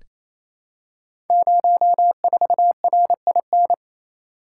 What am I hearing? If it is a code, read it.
Morse code, heard as a keyed tone, about 28 wpm, 700 Hz. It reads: 04RIN
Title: Code Groups - 5 Letter:Numbers - 28wpm